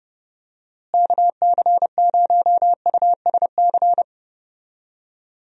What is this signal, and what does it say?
Morse code, heard as a keyed tone, about 31 wpm, 700 Hz. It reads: KC0USC
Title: USA Call Signs - 6 Characters - Repeated 3x - 30wpm